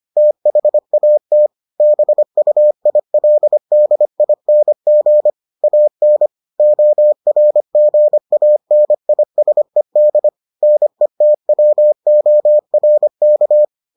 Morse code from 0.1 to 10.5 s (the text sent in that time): THAT BUILDING AN ORGANISED